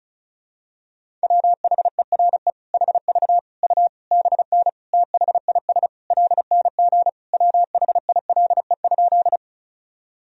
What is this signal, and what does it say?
Morse code, heard as a keyed tone, about 34 wpm, 700 Hz. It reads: WHERE HV U BN THIS LNG WHILE?
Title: Sentences from Top 300 Words and Abbreviations 35wpm